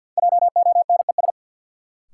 Morse code, heard as a keyed tone, about 49 wpm, 700 Hz. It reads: JONES
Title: Sets of 5 Words - New Word Drill - 50wpm